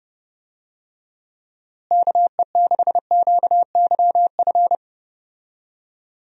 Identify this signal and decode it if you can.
Morse code, heard as a keyed tone, about 30 wpm, 700 Hz. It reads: KE6QYF
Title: USA Call Signs - 6 Characters - Repeated 3x - 30wpm